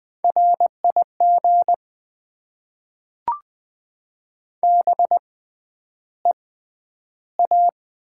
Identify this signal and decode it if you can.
Morse code, heard as a keyed tone, about 20 wpm, spaced as for 10 wpm, 700 Hz. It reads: RIG BEA
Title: Top 90 QSO Elements - 6x Character Spacing - 20wpm